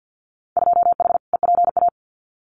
Morse code, heard as a keyed tone, about 50 wpm, 700 Hz. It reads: 2H EPA
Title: Winter Field Day Exchanges 50wpm